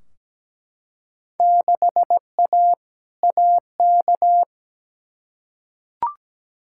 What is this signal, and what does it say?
Morse code, heard as a keyed tone, about 17 wpm, 700 Hz. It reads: 6A AK
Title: Field Day Exchanges Encore - 17wpm